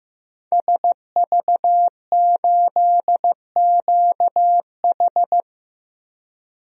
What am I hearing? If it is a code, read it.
Morse code, heard as a keyed tone, about 15 wpm, 700 Hz. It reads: SV8QH
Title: DXCC-100 Call Signs - 5 Characters - Repeated 2x - 15wpm